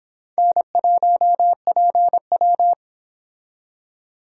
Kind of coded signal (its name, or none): Morse code